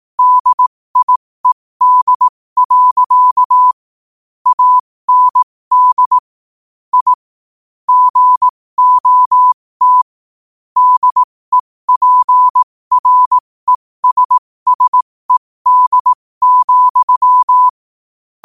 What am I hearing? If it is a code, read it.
Morse code, heard as a keyed tone, about 18 wpm, spaced as for 15 wpm, 1000 Hz. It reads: DIED. AND I GOT DEPRESSED,
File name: qotdfiles / 15wpm